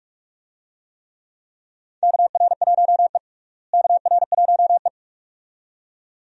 Morse code, heard as a keyed tone, about 42 wpm, 700 Hz. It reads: KR1E KR1E
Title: USA Call Signs - 4 Characters - Repeated 2x - 45wpm